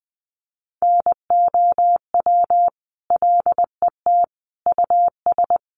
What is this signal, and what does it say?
Morse code, heard as a keyed tone, about 20 wpm, 700 Hz. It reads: NOW LET US
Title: Sentences from Top 800 Words 20wpm